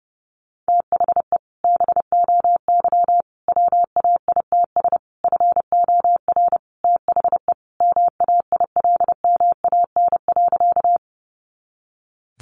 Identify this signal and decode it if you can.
Morse code, heard as a keyed tone, about 30 wpm, 700 Hz. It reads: THE BOY WAITS FOR THE MAILMAN.
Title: Easy Phrases 30wpm